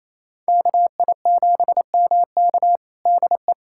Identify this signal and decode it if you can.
Morse code, heard as a keyed tone, about 28 wpm, 700 Hz. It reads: KI7MK DE
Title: USA Callsign DE Callsign - 5 Characters - 28wpm